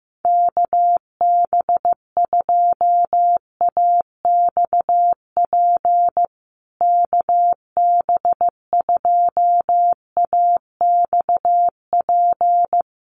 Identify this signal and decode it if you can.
Morse code, heard as a keyed tone, about 15 wpm, 700 Hz. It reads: KB2AXP KB2AXP
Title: USA Call Signs Repeated 2x 15wpm